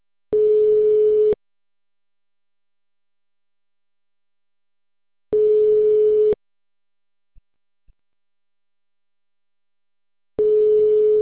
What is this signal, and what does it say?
Ringback tone (Europe, 425 Hz).